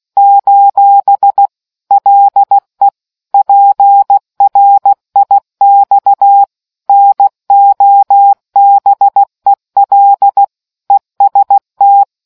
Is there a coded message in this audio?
Yes: Morse code from 0.2 to 12.0 s, heard as a keyed tone, about 16 wpm, 800 Hz. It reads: : LE PRIX NOBEL EST